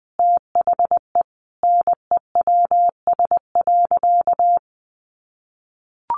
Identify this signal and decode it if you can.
Morse code, heard as a keyed tone, about 20 wpm, 700 Hz. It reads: THE NEWS.
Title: Sentences from Top 800 Words 20wpm